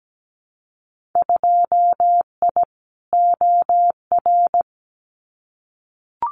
Morse code, heard as a keyed tone, about 17 wpm, 700 Hz. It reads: 2I OR
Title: Winter Field Day Exchanges Encore 17wpm